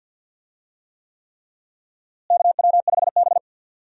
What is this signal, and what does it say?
Morse code, heard as a keyed tone, about 52 wpm, 700 Hz. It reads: KW5B
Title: USA Call Signs - 4 Characters - Repeated 3x - 50wpm